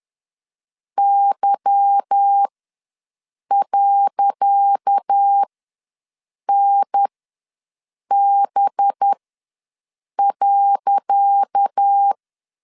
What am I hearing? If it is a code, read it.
Morse code, heard as a keyed tone, about 10 wpm, spaced as for 6 wpm, 800 Hz. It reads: Y.NB.
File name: LEZIONI / 10 SIGNS / 10C DOT ws 10wpm 4213